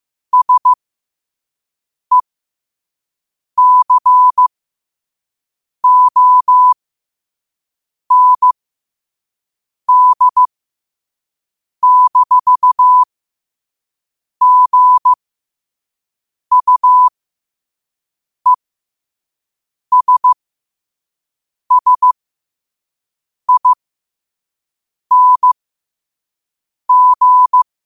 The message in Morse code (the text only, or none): SECOND-GUESSING